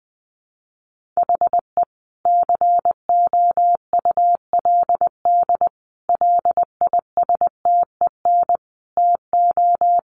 Morse code, heard as a keyed tone, about 19 wpm, 700 Hz. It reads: HE COULD LISTEN TO
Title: Sentences from Top 1500 Words 20wpm